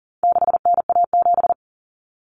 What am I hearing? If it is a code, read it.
Morse code, heard as a keyed tone, about 40 wpm, 700 Hz. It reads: 6NA7
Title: Common Cut Numbers - 4-Digit 40wpm